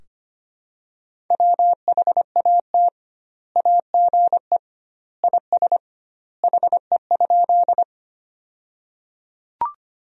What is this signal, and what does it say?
Morse code, heard as a keyed tone, about 25 wpm, 700 Hz. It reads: WHAT AGE IS HE?
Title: 2x Word-Spacing - Sentences from Top 500 Words - 25wpm